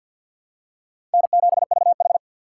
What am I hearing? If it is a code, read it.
Morse code, heard as a keyed tone, about 50 wpm, 700 Hz. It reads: N7VH
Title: USA Call Signs - 3x Character Spacing - 50wpm